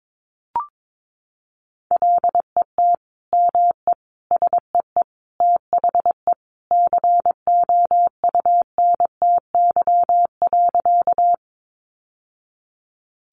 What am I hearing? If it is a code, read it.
Morse code, heard as a keyed tone, about 22 wpm, 700 Hz. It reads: LET ME SEE THE COUNTY.
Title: Sentences from Top 1200 Words 22wpm